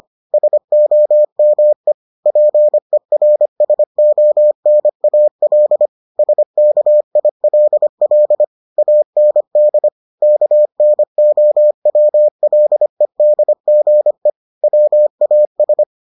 Morse code, heard as a keyed tone, about 25 wpm, 600 Hz. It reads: SOME PERSONAL SKILL AND KNOWLEDGE WAS